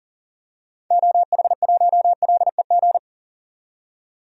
Morse code, heard as a keyed tone, about 40 wpm, 700 Hz. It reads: OH1LEG
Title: DXCC-100 Call Signs - 6 Characters - 40wpm